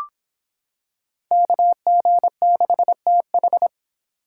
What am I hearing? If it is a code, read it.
Morse code, heard as a keyed tone, about 26 wpm, 700 Hz. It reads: KG6TH